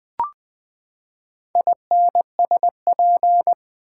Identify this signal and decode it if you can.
Morse code, heard as a keyed tone, about 21 wpm, 700 Hz. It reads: INSP
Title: Build A Word - 4 Letters At A Time - 11-Letter Words - 20wpm